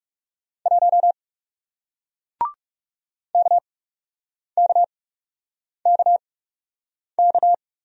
Morse code, heard as a keyed tone, about 40 wpm, spaced as for 8 wpm, 700 Hz. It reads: 1 KKKK